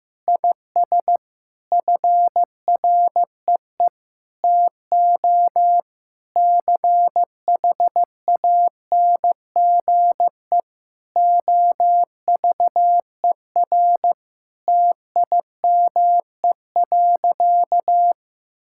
Morse code, heard as a keyed tone, about 15 wpm, 700 Hz. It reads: IS FREE TO CHANGE OVER TIME.
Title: Sentences from Top 1000 Words - Review Part 6 - 15wpm